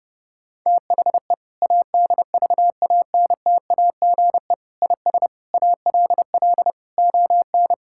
Morse code, heard as a keyed tone, about 32 wpm, 700 Hz. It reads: THE ADVANTAGE IS ALL ON
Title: Sentences from Top 1000 Words - Review Part 6 - 30wpm